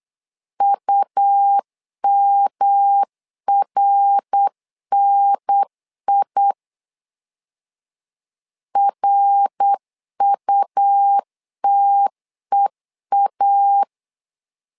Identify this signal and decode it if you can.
Morse code, heard as a keyed tone, about 8 wpm, 800 Hz. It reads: UMRNI RUTEA
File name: LEZIONI / 3 MWRO / 3 MWR 8wpm 2444